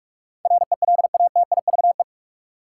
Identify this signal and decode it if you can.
Morse code, heard as a keyed tone, about 47 wpm, 700 Hz. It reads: RELATIVE
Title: Sentences from Top 1400 Words - New Word Drill Alt - 45wpm